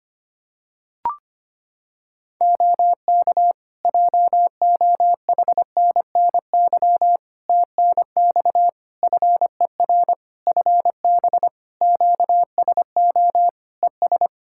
Morse code, heard as a keyed tone, about 25 wpm, 700 Hz. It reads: OK JOHNNY TNX FER FB QSO ES
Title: Practice QSO - Exchange 4 - Ending - 25wpm